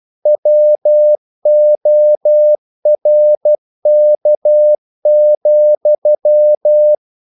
Morse code, heard as a keyed tone, about 12 wpm, 600 Hz. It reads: WORK,
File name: CW / Practice / EnglishQuotations0112